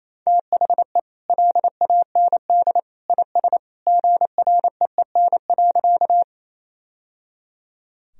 Morse code, heard as a keyed tone, about 28 wpm, 700 Hz. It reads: THE LAND IS GREEN.
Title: Sentences from Top 1000 Words - Review Part 2 - 28wpm